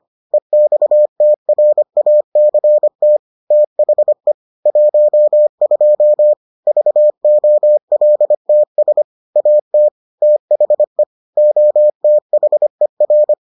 Morse code, heard as a keyed tone, about 25 wpm, 600 Hz. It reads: EXTRACT THE 12 VOLTS AT THE OTHER